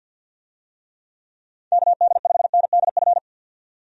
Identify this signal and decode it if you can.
Morse code, heard as a keyed tone, about 51 wpm, 700 Hz. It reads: KD5NDF
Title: USA Call Signs - 6 Characters - 50wpm